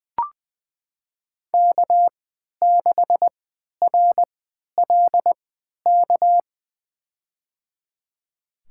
Morse code, heard as a keyed tone, about 20 wpm, spaced as for 11 wpm, 700 Hz. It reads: K6RLK
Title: USA Call Signs - 5 Characters - 3x Character Spacing - 20wpm